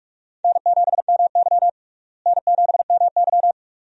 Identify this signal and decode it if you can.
Morse code, heard as a keyed tone, about 46 wpm, 700 Hz. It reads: N7MY N7MY
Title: USA Call Signs - 4 Characters - Repeated 2x - 45wpm